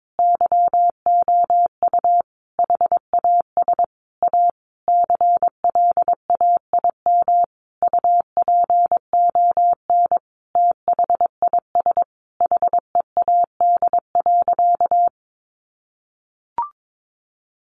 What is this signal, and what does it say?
Morse code, heard as a keyed tone, about 22 wpm, 700 Hz. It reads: YOU HAS A CLAIM UPON THIS HEAD.
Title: Sentences from Top 1500 Words 22wpm